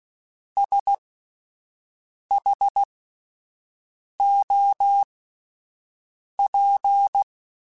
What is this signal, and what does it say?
Morse code, heard as a keyed tone, about 16 wpm, spaced as for 5 wpm, 800 Hz. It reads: SHOP
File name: Spaced Characters / Spaced Characters-mp3s / Weather4 24 sp @10